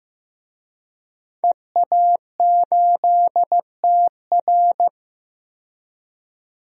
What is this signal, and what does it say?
Morse code, heard as a keyed tone, about 15 wpm, 700 Hz. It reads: EA8TR
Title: DXCC-100 Call Signs - 5 Characters - 15wpm